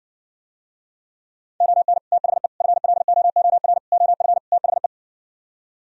Morse code, heard as a keyed tone, about 60 wpm, 700 Hz. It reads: ON THE FLOOR OF THE